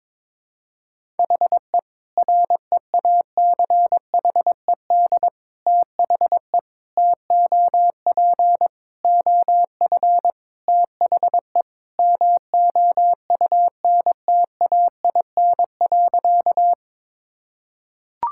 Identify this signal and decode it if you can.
Morse code, heard as a keyed tone, about 22 wpm, 700 Hz. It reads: HE REACHED THE TOP OF THE MOUNTAIN.
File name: practice / 22wpm / Sentences with 3-Word Phrases - 22wpm